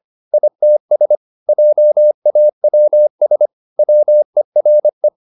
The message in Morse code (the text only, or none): ITS JAWS WERE